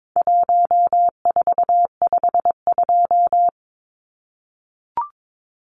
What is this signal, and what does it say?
Morse code, heard as a keyed tone, about 22 wpm, 700 Hz. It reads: 1452
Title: Numbers - Contest Style - 4-Digit 22wpm